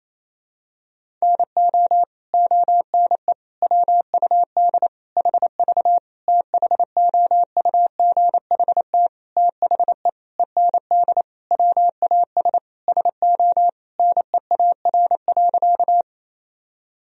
Morse code, heard as a keyed tone, about 28 wpm, 700 Hz. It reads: NO ONE WUD HV THOUGHT THE END WAS SO NEAR.
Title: Sentences from Top 300 Words and Abbreviations 28wpm